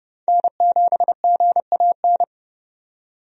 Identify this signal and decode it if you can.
Morse code, heard as a keyed tone, about 31 wpm, 700 Hz. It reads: N7GAN
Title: USA Call Signs - 5 Characters - 30wpm